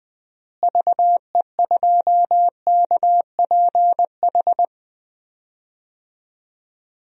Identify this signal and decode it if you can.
Morse code, heard as a keyed tone, about 20 wpm, 700 Hz. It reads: VE2KPH
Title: DXCC-100 Call Signs - 6 Characters - Repeated 2x - 20wpm